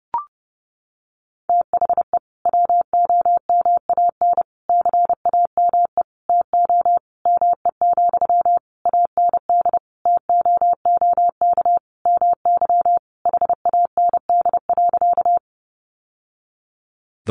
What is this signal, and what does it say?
Morse code, heard as a keyed tone, about 30 wpm, 700 Hz. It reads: THE WOMAN CAME TO ME, AND TOOK MY HAND.
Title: Sentences from Top 1100 Words 30wpm